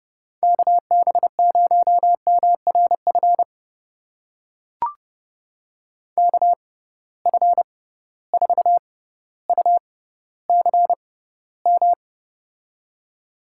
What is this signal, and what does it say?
Morse code, heard as a keyed tone, about 30 wpm, spaced as for 15 wpm, 700 Hz. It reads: KB0MRF KF4UCM
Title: USA Call Signs - 6 Characters - 6x Character Spacing - 30wpm